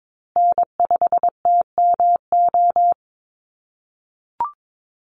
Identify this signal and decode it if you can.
Morse code, heard as a keyed tone, about 22 wpm, 700 Hz. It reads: N5TMO
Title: USA Call Signs Repeated 3x 22wpm